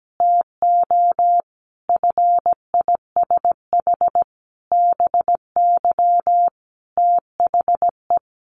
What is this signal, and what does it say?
Morse code, heard as a keyed tone, about 17 wpm, 700 Hz. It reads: TO FISH BY THE
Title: Sentences from Top 1400 Words 17wpm